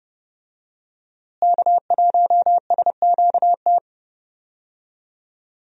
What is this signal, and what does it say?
Morse code, heard as a keyed tone, about 30 wpm, 700 Hz. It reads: K1SQT
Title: USA Call Signs - 5 Characters - Repeated 3x - 30wpm